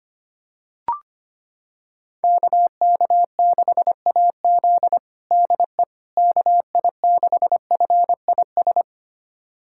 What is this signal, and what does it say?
Morse code, heard as a keyed tone, about 25 wpm, 700 Hz. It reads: KK6AZ DE KI6FIS
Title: USA Callsign DE Callsign 25wpm